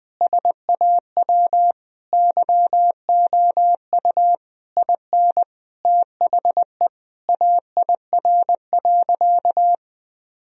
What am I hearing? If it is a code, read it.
Morse code, heard as a keyed tone, about 21 wpm, 700 Hz. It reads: SAW YOU IN THE AIR.
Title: Sentences from Top 1000 Words - Review Part 1 - 20wpm